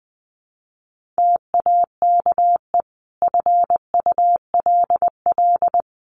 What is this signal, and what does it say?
Morse code, heard as a keyed tone, about 20 wpm, 700 Hz. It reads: TAKE FULL
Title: Sentences from Top 1400 Words 20wpm